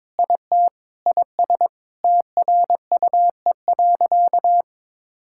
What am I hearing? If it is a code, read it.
Morse code, heard as a keyed tone, about 22 wpm, 700 Hz. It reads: IT IS TRUE.
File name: practice / 22wpm / Sentences with 4-Word Phrases - 22wpm